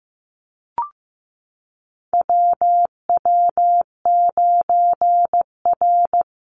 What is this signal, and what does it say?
Morse code, heard as a keyed tone, about 15 wpm, 700 Hz. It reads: WW9R
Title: Call Signs Repeated 3x - 15wpm